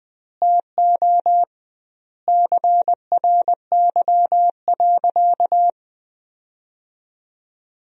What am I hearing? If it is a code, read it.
Morse code, heard as a keyed tone, about 20 wpm, 700 Hz. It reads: TO CRY.
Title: 2x Word-Spacing - Sentences from Top 500 Words - 20wpm